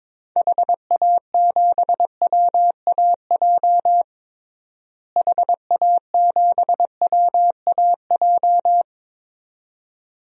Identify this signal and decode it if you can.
Morse code, heard as a keyed tone, about 22 wpm, 700 Hz. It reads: HA7WAJ HA7WAJ
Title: DXCC-100 Call Signs - 6 Characters - Repeated 2x - 22wpm